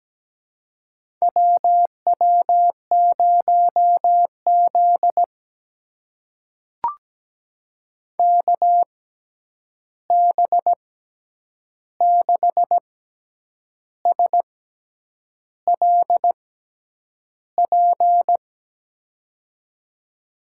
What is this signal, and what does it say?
Morse code, heard as a keyed tone, about 17 wpm, spaced as for 8 wpm, 700 Hz. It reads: WW0Z KB6SLP
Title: USA Call Signs - 6x Character Spacing - 17wpm